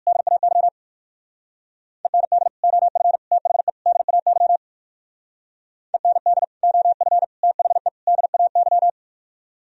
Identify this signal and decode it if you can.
Morse code, heard as a keyed tone, about 62 wpm, 700 Hz. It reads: DAY END OF THE DAY END OF THE DAY